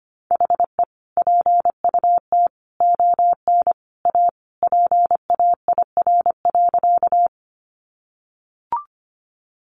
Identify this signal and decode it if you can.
Morse code, heard as a keyed tone, about 25 wpm, 700 Hz. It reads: HE PUT ON A PAIR.
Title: Sentences from Top 1200 Words 25wpm